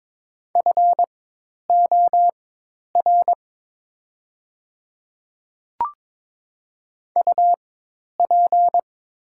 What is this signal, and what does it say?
Morse code, heard as a keyed tone, about 21 wpm, spaced as for 10 wpm, 700 Hz. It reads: FOR UP
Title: 4x Character Spacing - Top 100 Words - 22wpm